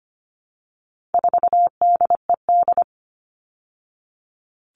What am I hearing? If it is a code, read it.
Morse code, heard as a keyed tone, about 25 wpm, 700 Hz. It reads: 4DED
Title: All Cut Numbers - 4-Digit 25wpm